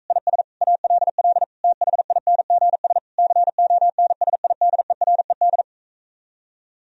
Morse code, heard as a keyed tone, about 45 wpm, 700 Hz. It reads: IS ALL THINGS CONSIDERED